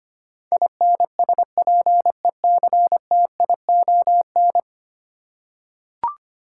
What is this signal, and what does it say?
Morse code, heard as a keyed tone, about 25 wpm, 690 Hz. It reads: INSPECTION
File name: 10 Letter Words - First 3 Letters Given - 25wpm